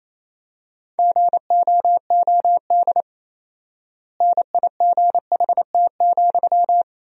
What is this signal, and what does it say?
Morse code, heard as a keyed tone, about 28 wpm, 700 Hz. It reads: GOOD NIGHT,
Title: 4x Word-Spacing - Sentences from Top 300 Words - 28wpm